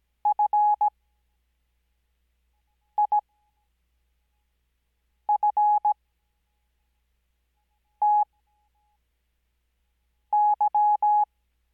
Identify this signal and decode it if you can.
Morse code, heard as a keyed tone, about 17 wpm, spaced as for 4 wpm, 840 Hz. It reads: FIFTY
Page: Farnsworth Spacing: 15WPM Character speed Spaced at 5WPM.